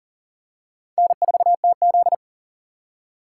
Morse code, heard as a keyed tone, about 40 wpm, 700 Hz. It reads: N4TZ
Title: K1USN SST QSO Practice - 40wpm